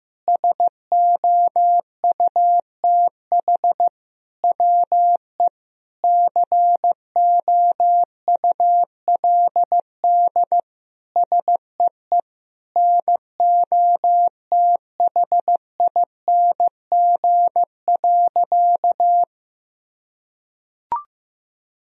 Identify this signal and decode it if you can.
Morse code, heard as a keyed tone, about 15 wpm, 700 Hz. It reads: SOUTH WE COULD SEE NOTHING.
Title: Sentences from Top 1000 Words - Review Part 2 - 15wpm